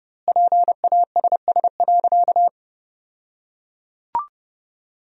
Morse code, heard as a keyed tone, about 32 wpm, 700 Hz. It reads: PASS.
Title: 2x Word-Spacing - Sentences from Top 400 Words - 30wpm